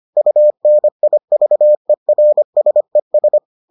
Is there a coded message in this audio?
Yes: Morse code from 0.2 to 3.4 s, heard as a keyed tone, about 25 wpm, 600 Hz. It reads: UNIVERSES